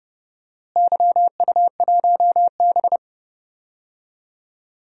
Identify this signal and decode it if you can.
Morse code, heard as a keyed tone, about 30 wpm, 700 Hz. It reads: YU1B
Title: DXCC-100 Call Signs - 4 Characters - Repeated 3x - 30wpm